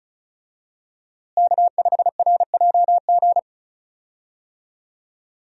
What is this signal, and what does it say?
Morse code, heard as a keyed tone, about 34 wpm, 700 Hz. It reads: K5RJG
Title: USA Call Signs - 5 Characters - Repeated 3x - 35wpm